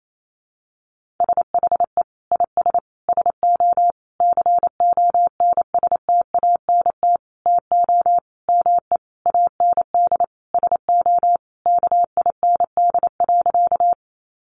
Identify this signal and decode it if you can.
Morse code, heard as a keyed tone, about 28 wpm, 700 Hz. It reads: SHE IS SO CONSTANT TO ME AND SO KIND.
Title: Sentences from Top 1500 Words 28wpm